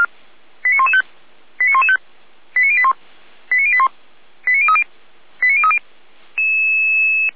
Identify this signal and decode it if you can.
Five-tone selective call (ZVEI), heard as quick runs of five tones, tones of about 80 ms. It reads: 89115 89115 89081 89081 89030 89030